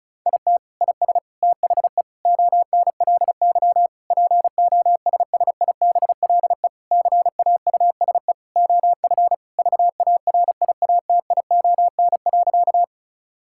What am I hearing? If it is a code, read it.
Morse code, heard as a keyed tone, about 36 wpm, 700 Hz. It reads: IT IS THE ONLY POSSIBLE CAUSE OF VARIATION.
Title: Sentences from Top 1600 Words 35wpm